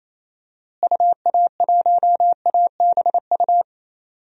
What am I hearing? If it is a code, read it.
Morse code, heard as a keyed tone, about 28 wpm, 700 Hz. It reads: UA1ABU
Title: DXCC-100 Call Signs - 6 Characters - Repeated 2x - 28wpm